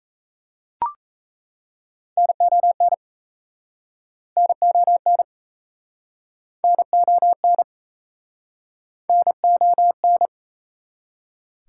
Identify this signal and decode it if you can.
Morse code, heard as a keyed tone, about 38 wpm, 690 Hz. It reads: NON NON NON NON